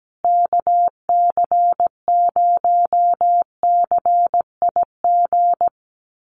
Morse code, heard as a keyed tone, about 17 wpm, 700 Hz. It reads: KC0CIG
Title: USA Call Signs Repeated 2x 17wpm